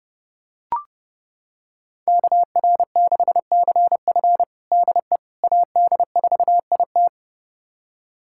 Morse code, heard as a keyed tone, about 30 wpm, 700 Hz. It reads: KR6CF DE AD4IT
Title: USA Callsign DE Callsign - 5 Characters - 30wpm